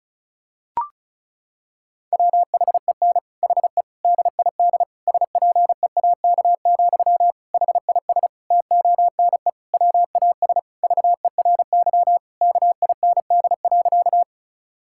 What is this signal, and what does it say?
Morse code, heard as a keyed tone, about 36 wpm, 700 Hz. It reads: WHEN HE DID SPEAK, HIS TONE WAS VERY KIND.
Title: Sentences from Top 1600 Words 35wpm